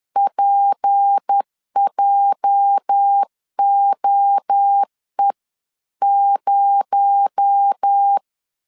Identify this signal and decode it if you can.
Morse code, heard as a keyed tone, about 10 wpm, 800 Hz. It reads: PJOE0
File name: LEZIONI / 8 YJ45 / 8 YJ 10wpm 2651